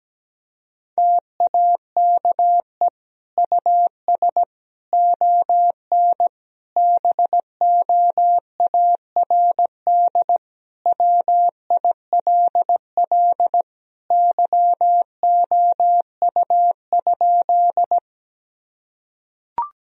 Morse code, heard as a keyed tone, about 17 wpm, 700 Hz. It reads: TAKE US ON BOARD WILL YOU?
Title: Sentences from Top 1000 Words - Review Part 4 - 17wpm